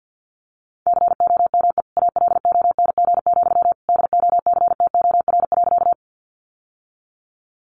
Morse code, heard as a keyed tone, about 50 wpm, 700 Hz. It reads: COME ALONG, DOCTOR.
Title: Sentences from Top 1500 Words 50wpm